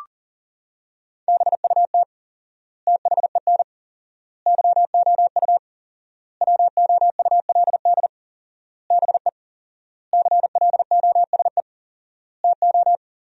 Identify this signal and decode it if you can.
Morse code, heard as a keyed tone, about 40 wpm, 700 Hz. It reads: BUT THEN YOU WOULD BE CLOSE TO
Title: 4x Word-Spacing - Sentences from Top 300 Words - 40wpm